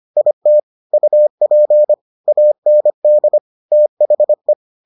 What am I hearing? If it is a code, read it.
Morse code, heard as a keyed tone, about 25 wpm, 600 Hz. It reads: IT UP AND THE